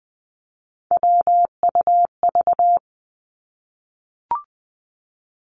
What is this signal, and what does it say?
Morse code, heard as a keyed tone, about 20 wpm, 700 Hz. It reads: WUV
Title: Three Challenging Letter-Number Combinations 20wpm